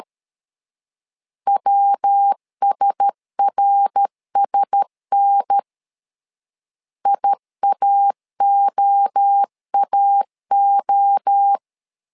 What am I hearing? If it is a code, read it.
Morse code, heard as a keyed tone, about 13 wpm, 800 Hz. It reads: WSRSN IAOAO
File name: LEZIONI / 4 Q1S 12wpm 1802